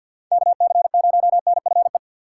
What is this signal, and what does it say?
Morse code, heard as a keyed tone, about 48 wpm, 700 Hz. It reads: KC0NFE
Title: USA Call Signs - 6 Characters - 3x Character Spacing - 50wpm